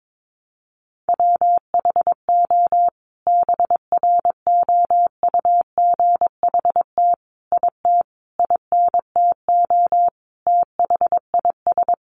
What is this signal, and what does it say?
Morse code, heard as a keyed tone, about 22 wpm, 700 Hz. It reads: WHO BROUGHT IT INTO THIS
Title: Sentences from Top 600 Words 22wpm